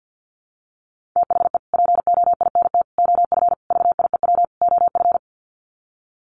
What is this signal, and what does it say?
Morse code, heard as a keyed tone, about 48 wpm, 700 Hz. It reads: THE POINT OF VIEW OF
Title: Sets of 5 Words 50wpm